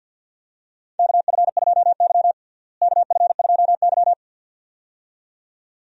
Morse code, heard as a keyed tone, about 52 wpm, 700 Hz. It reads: KF2Y KF2Y
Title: USA Call Signs - 4 Characters - Repeated 2x - 50wpm